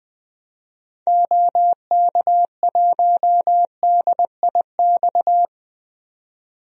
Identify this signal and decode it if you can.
Morse code, heard as a keyed tone, about 21 wpm, 700 Hz. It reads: OK1DIX
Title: DXCC-100 Call Signs - 6 Characters - 20wpm